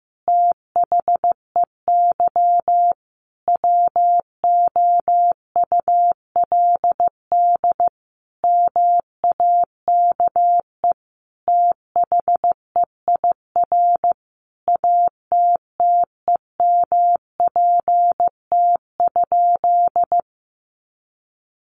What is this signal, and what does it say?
Morse code, heard as a keyed tone, about 15 wpm, 700 Hz. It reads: THEY WOULD MAKE THEIR ATTEMPT?
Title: Sentences from Top 1000 Words 15wpm